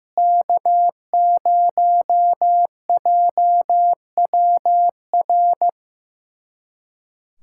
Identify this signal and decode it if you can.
Morse code, heard as a keyed tone, about 15 wpm, 700 Hz. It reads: K0JWR
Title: USA Callsign DE Callsign - 5 Characters - 15wpm